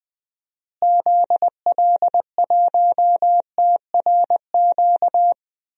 Morse code, heard as a keyed tone, about 20 wpm, 700 Hz. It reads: ZL1TRQ
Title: DXCC-100 Call Signs - 6 Characters - Repeated 2x - 20wpm